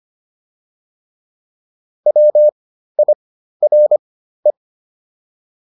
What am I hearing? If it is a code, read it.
Morse code, heard as a keyed tone, about 25 wpm, spaced as for 13 wpm, 600 Hz. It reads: WIRE